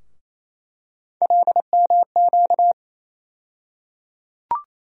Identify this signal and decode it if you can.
Morse code, heard as a keyed tone, about 28 wpm, 700 Hz. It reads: LMQ
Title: USA Call Sign Postfixes - 3 Letters 28wpm